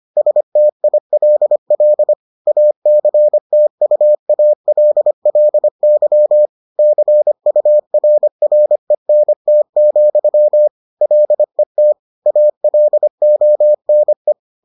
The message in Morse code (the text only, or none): STILL ACTUALLY CURRENT, LET ALONE